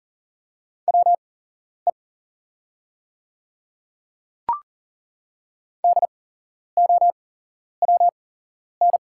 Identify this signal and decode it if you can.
Morse code, heard as a keyed tone, about 40 wpm, spaced as for 11 wpm, 700 Hz. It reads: WE DOWN
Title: 8x Character Spacing - Top 200 Words - 40wpm